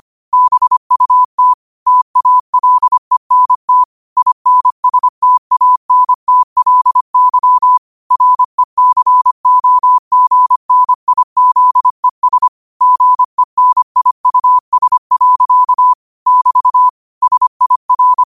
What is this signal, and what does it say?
Morse code, heard as a keyed tone, about 25 wpm, 1000 Hz. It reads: BUT TALENT INSTANTLY RECOGNIZES GENIUS. <BT> SIR
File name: qotdfiles / 25wpm